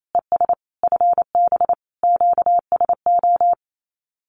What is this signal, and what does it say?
Morse code, heard as a keyed tone, about 28 wpm, 700 Hz. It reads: ES FB QSO
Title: Common QSO Phrases Encore 28wpm